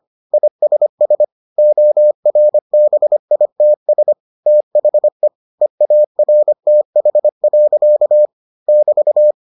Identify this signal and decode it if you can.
Morse code, heard as a keyed tone, about 25 wpm, 600 Hz. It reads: ISS ORBITS THE EARTH. <BT>